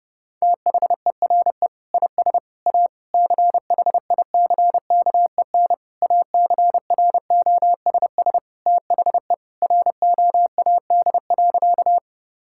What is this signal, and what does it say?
Morse code, heard as a keyed tone, about 30 wpm, 700 Hz. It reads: THERE IS A CHICKEN ACROSS THE ROAD.
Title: Sentences from Top 1000 Words - Review Part 5 - 30wpm